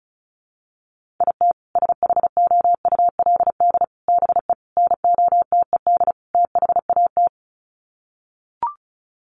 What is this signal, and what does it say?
Morse code, heard as a keyed tone, about 34 wpm, 700 Hz. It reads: IT SHOULD BE NOTED THAT
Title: Sets of 5 Words 35wpm